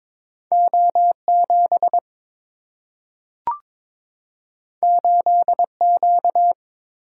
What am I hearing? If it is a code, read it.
Morse code, heard as a keyed tone, about 22 wpm, 700 Hz. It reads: O7 8Q
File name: practice / 22wpm / Two Letter-Number Combinations - 22wpm